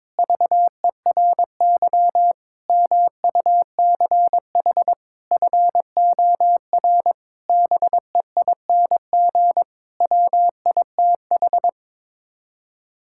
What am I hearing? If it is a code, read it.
Morse code, heard as a keyed tone, about 22 wpm, 700 Hz. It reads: VERY MUCH FOR BEING WITH
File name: Sets of 9 Words - 22wpm